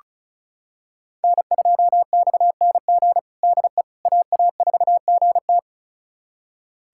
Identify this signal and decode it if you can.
Morse code, heard as a keyed tone, about 34 wpm, 690 Hz. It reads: N2XNG DE AA4GT
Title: USA Callsign DE Callsign - 5 Characters - 35wpm